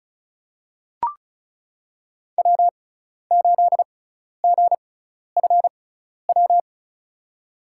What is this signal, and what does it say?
Morse code, heard as a keyed tone, about 36 wpm, spaced as for 12 wpm, 700 Hz. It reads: W8GFW
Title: USA Call Signs - 5 Characters - 6x Character Spacing - 35wpm